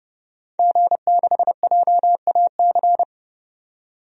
Morse code, heard as a keyed tone, about 30 wpm, 700 Hz. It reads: G6JAC
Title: DXCC-100 Call Signs - 5 Characters - 30wpm